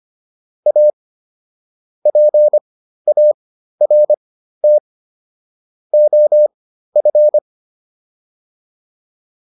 Morse code, heard as a keyed tone, about 25 wpm, spaced as for 13 wpm, 600 Hz. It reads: A PART OF
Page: intermediate-practice-files